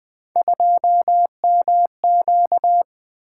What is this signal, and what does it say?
Morse code, heard as a keyed tone, about 20 wpm, 700 Hz. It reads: 2MQ
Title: Three Letter-Number Combinations - 20wpm